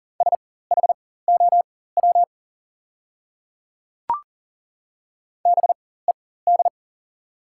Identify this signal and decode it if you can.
Morse code, heard as a keyed tone, about 40 wpm, spaced as for 19 wpm, 690 Hz. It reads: SHOW BED
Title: 4x Character Spacing - Top 500 Words - 40wpm